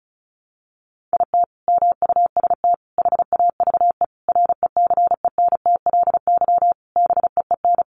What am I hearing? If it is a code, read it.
Morse code, heard as a keyed tone, about 34 wpm, 700 Hz. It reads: IT MUST HAVE RECENTLY BEEN
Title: Sentences from Top 1200 Words 35wpm